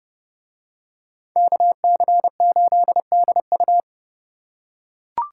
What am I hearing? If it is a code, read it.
Morse code, heard as a keyed tone, about 30 wpm, 700 Hz. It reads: KC8DU